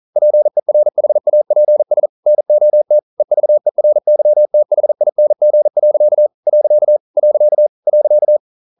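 Morse code, heard as a keyed tone, about 42 wpm, 600 Hz. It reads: PERHAPS NOT EVERYTHING. . .
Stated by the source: Morse code at 40 wpm, practice text, CW med Gnister.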